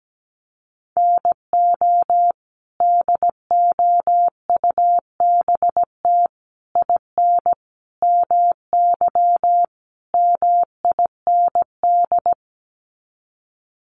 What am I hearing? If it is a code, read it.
Morse code, heard as a keyed tone, about 17 wpm, 700 Hz. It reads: NO DOUBT IN MY MIND
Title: Sets of 5 Words Encore2 17wpm